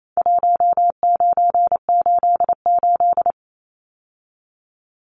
Morse code, heard as a keyed tone, about 28 wpm, 700 Hz. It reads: 1988
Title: Numbers - Contest Style - 4-Digit 28wpm